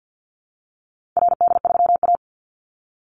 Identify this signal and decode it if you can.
Morse code, heard as a keyed tone, about 50 wpm, 700 Hz. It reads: RD3A
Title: Call Signs Repeated 2x - 50wpm